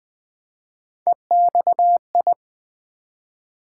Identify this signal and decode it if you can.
Morse code, heard as a keyed tone, about 20 wpm, 700 Hz. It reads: EXI
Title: Build A Word - 1 Letter At A Time - 5-Letter Words - 20wpm